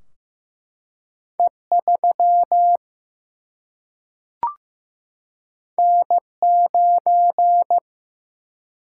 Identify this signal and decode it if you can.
Morse code, heard as a keyed tone, about 15 wpm, 700 Hz. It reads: E3 N9
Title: Single-Letter Single-Number 15wpm